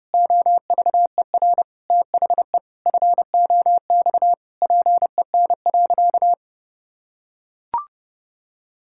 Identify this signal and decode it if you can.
Morse code, heard as a keyed tone, about 32 wpm, 700 Hz. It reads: OVER THE FOX PEN.
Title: Pangram Challenge v2 - 30wpm